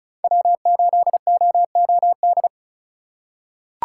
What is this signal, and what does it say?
Morse code, heard as a keyed tone, about 34 wpm, 700 Hz. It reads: W8OOD
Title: USA Call Signs - 5 Characters - 3x Character Spacing - 35wpm